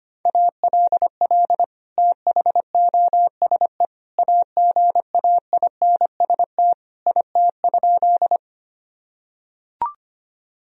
Morse code, heard as a keyed tone, about 25 wpm, 700 Hz. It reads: ALL THOSE AGAINST IT?
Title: Sentences from Top 1000 Words - Review Part 1 - 25wpm